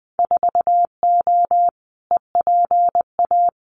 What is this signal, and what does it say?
Morse code, heard as a keyed tone, about 20 wpm, 700 Hz. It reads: 4O EPA
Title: Winter Field Day Exchanges 20wpm